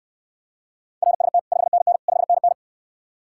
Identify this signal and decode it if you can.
Morse code, heard as a keyed tone, about 68 wpm, 700 Hz. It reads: RST 5NN 5NN